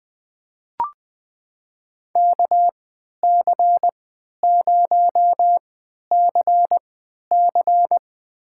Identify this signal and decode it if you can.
Morse code, heard as a keyed tone, about 20 wpm, spaced as for 11 wpm, 700 Hz. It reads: KC0CC
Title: USA Call Signs - 5 Characters - 3x Character Spacing - 20wpm